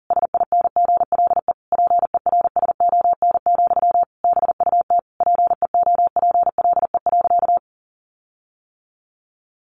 Morse code, heard as a keyed tone, about 37 wpm, 700 Hz. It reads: SINGLE PERSON, BUT PEOPLE.
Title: Sentences from Top 1400 Words 40wpm